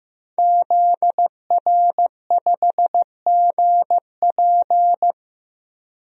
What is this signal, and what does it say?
Morse code, heard as a keyed tone, about 15 wpm, 700 Hz. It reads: ZR5GP
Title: DXCC-100 Call Signs - 5 Characters - 15wpm